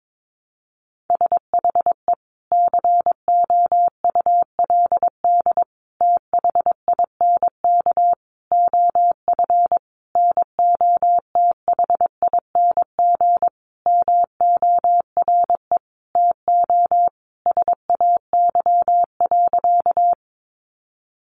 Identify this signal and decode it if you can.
Morse code, heard as a keyed tone, about 22 wpm, 700 Hz. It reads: SHE COULD THINK OF NOTHING MORE TO SAY.
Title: Sentences from Top 500 Words 22wpm